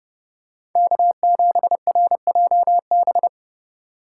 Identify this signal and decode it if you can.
Morse code, heard as a keyed tone, about 31 wpm, 700 Hz. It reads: K7RJB